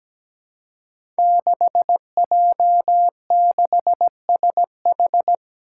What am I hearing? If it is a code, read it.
Morse code, heard as a keyed tone, about 17 wpm, 700 Hz. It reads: 6J6SH
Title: Five Challenging Letter-Number Combinations 17wpm